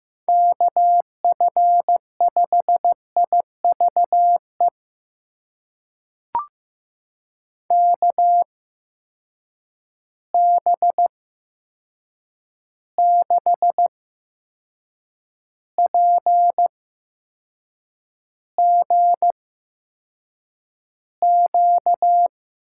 Morse code, heard as a keyed tone, about 15 wpm, spaced as for 6 wpm, 700 Hz. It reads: KF5IVE KB6PGQ
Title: USA Call Signs - 6 Characters - 8x Character Spacing - 15wpm